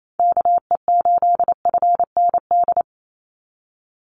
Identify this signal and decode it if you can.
Morse code, heard as a keyed tone, about 28 wpm, 700 Hz. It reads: KE8FND
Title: USA Call Signs 28wpm